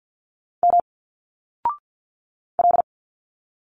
Morse code, heard as a keyed tone, about 50 wpm, 700 Hz. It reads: M L
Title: Single Letters 50wpm